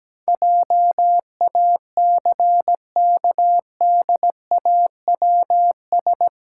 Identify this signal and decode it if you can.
Morse code, heard as a keyed tone, about 17 wpm, 700 Hz. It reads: JACKDAWS
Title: Pangram Challenge v2 - 17wpm